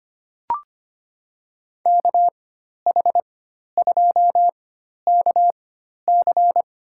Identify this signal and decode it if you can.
Morse code, heard as a keyed tone, about 25 wpm, spaced as for 12 wpm, 700 Hz. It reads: KH2KC
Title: USA Call Signs - 5 Characters - 4x Character Spacing - 25wpm